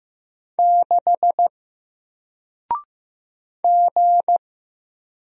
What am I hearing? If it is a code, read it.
Morse code, heard as a keyed tone, about 15 wpm, 700 Hz. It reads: 6 G
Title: Single Letter-Number Alt 15wpm